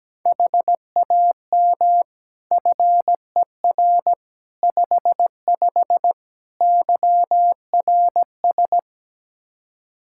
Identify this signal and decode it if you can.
Morse code, heard as a keyed tone, about 17 wpm, 700 Hz. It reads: HAM FER 55 YRS
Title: K1USN SST QSO Practice 17wpm